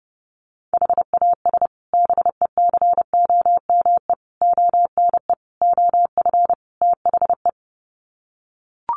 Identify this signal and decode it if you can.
Morse code, heard as a keyed tone, about 29 wpm, 700 Hz. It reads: HAS BECOME ONE OF THE
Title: Sets of 5 Words Encore 30wpm